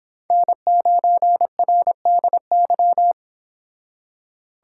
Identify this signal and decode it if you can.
Morse code, heard as a keyed tone, about 26 wpm, 700 Hz. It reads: N9RDY